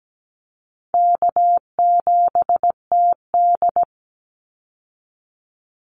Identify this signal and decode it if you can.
Morse code, heard as a keyed tone, about 17 wpm, 700 Hz. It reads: K7TD
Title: Call Signs - 17wpm